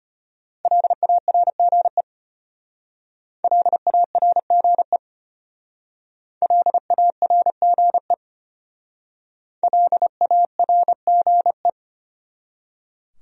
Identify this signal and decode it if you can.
Morse code, heard as a keyed tone, about 31 wpm, 700 Hz. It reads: LARGE LARGE LARGE LARGE
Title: Speed-Racing Top 500 Words 25wpm